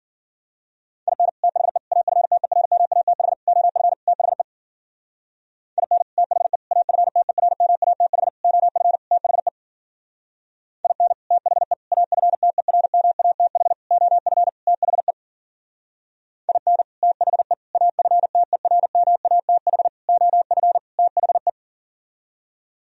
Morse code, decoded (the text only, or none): IN THE AFTERMATH OF THE IN THE AFTERMATH OF THE IN THE AFTERMATH OF THE IN THE AFTERMATH OF THE